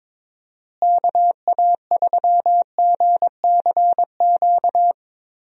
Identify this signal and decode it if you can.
Morse code, heard as a keyed tone, about 22 wpm, 700 Hz. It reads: KA3GCQ
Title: USA Call Signs - 6 Characters - Repeated 3x - 22wpm